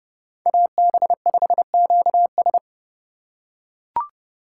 Morse code, heard as a keyed tone, about 28 wpm, 700 Hz. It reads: AB5QS